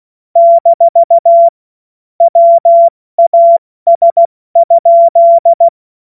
Morse code, heard as a keyed tone, about 16 wpm, spaced as for 14 wpm, 680 Hz. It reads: - WAS?
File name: Morsekurs / Witze / Witz 30. Dezember 2025 70 F80